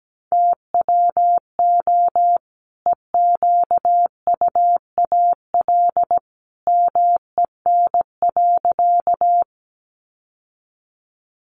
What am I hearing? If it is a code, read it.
Morse code, heard as a keyed tone, about 17 wpm, 700 Hz. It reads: TWO EQUAL MEN.
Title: Sentences from Top 1400 Words 17wpm